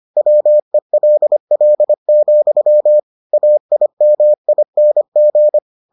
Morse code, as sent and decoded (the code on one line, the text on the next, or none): .-- . .-.. .-.. --..-- / .- .. -- .. -. --.
WELL, AIMING